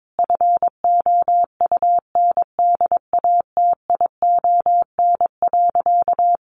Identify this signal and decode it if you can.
Morse code, heard as a keyed tone, about 22 wpm, 700 Hz. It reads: FOUNDATION.
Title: Sentences from Top 1200 Words 22wpm